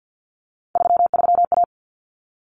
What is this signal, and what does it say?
Morse code, heard as a keyed tone, about 50 wpm, 700 Hz. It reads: 33A
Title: Common Cut Numbers - 3-Digit 50wpm